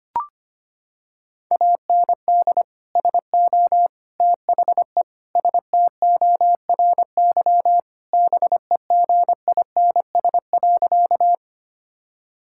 Morse code, heard as a keyed tone, about 25 wpm, 700 Hz. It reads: AND SO THE STORY BEGINS.
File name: practice / 25wpm / Sentences with 2-Word Phrases - 25wpm